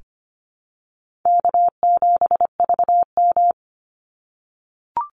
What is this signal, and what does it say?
Morse code, heard as a keyed tone, about 25 wpm, 700 Hz. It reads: K7VM
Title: Call Signs - 25wpm